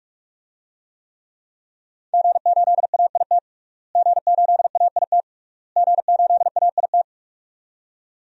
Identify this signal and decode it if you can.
Morse code, heard as a keyed tone, about 46 wpm, 700 Hz. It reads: G8AIT G8AIT G8AIT
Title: DXCC-100 Call Signs - 5 Characters - Repeated 3x - 45wpm